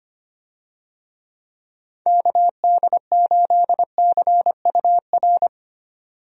Morse code, heard as a keyed tone, about 25 wpm, 700 Hz. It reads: KD8CUR
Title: USA Call Signs - 6 Characters - 25wpm